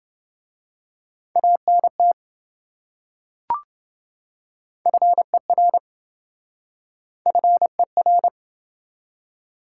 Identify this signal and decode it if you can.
Morse code, heard as a keyed tone, about 26 wpm, 700 Hz. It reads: ANT FER FER